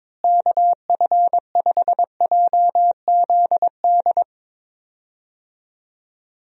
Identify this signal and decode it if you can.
Morse code, heard as a keyed tone, about 22 wpm, 700 Hz. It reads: KF5JZD
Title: USA Call Signs - 6 Characters - Repeated 2x - 22wpm